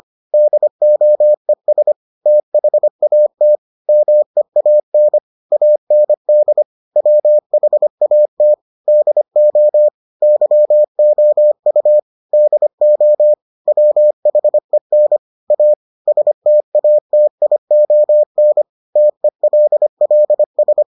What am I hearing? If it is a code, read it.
Morse code, heard as a keyed tone, about 25 wpm, 600 Hz. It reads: DOES THAT MEAN AND WHAT DO YOU DO WHEN A STATION TELLS